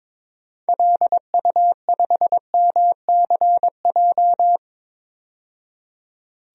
Morse code, heard as a keyed tone, about 22 wpm, 700 Hz. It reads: LU5MCJ
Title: DXCC-100 Call Signs - 6 Characters - 22wpm